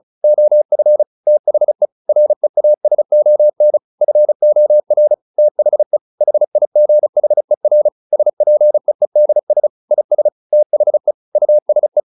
Morse code, heard as a keyed tone, about 34 wpm, 600 Hz. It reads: OF THE REASON FOR THE HIGHER SPEEDS IS THE USE